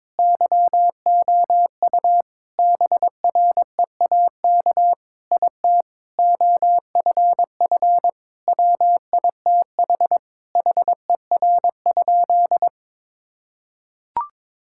Morse code, heard as a keyed tone, about 22 wpm, 700 Hz. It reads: YOU BREAK IT OFF WITH HER?
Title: Sentences from Top 1000 Words - Review Part 6 - 22wpm